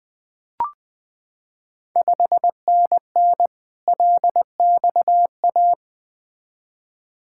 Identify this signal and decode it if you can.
Morse code, heard as a keyed tone, about 21 wpm, 700 Hz. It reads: 5NN LXA
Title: Ontario QSO Party Exchange 20wpm